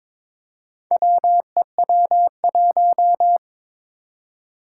Morse code, heard as a keyed tone, about 22 wpm, 700 Hz. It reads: WEW1
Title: Four Challenging Letter-Number Combinations 22wpm